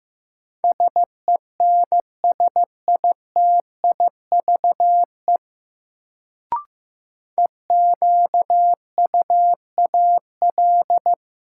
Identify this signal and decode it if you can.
Morse code, heard as a keyed tone, about 16 wpm, 700 Hz. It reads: SENSITIVE EQUAL
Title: Sentences from Top 1400 Words - New Word Drill Alt - 15wpm